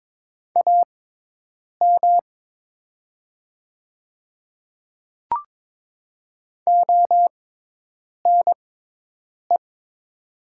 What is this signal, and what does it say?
Morse code, heard as a keyed tone, about 22 wpm, spaced as for 8 wpm, 700 Hz. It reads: AM ONE
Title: 6x Character Spacing - Top 500 Words - 22wpm